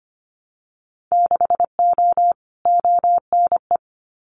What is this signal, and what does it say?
Morse code, heard as a keyed tone, about 25 wpm, 700 Hz. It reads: 6O ONE
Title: Winter Field Day Exchanges Encore 25wpm